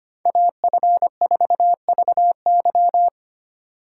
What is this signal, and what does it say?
Morse code, heard as a keyed tone, about 32 wpm, 700 Hz. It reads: AF4VY